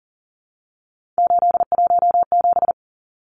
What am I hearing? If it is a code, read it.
Morse code, heard as a keyed tone, about 40 wpm, 700 Hz. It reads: 817
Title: Numbers - 3-Digit 40wpm